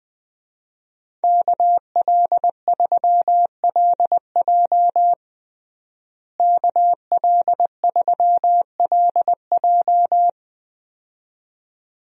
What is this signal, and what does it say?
Morse code, heard as a keyed tone, about 20 wpm, 700 Hz. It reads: KL3LJ KL3LJ
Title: USA Call Signs - 5 Characters - Repeated 2x - 20wpm